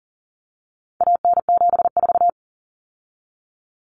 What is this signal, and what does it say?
Morse code, heard as a keyed tone, about 40 wpm, 700 Hz. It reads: AN74
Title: Common Cut Numbers - 4-Digit 40wpm